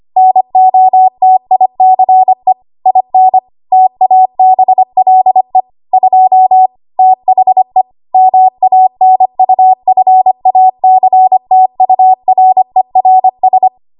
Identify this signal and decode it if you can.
Morse code, heard as a keyed tone, about 25 wpm, 750 Hz. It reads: NOTICE IN TABLE 2 THE MANUFACTURERS
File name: Archive / 25 WPM / 25WPM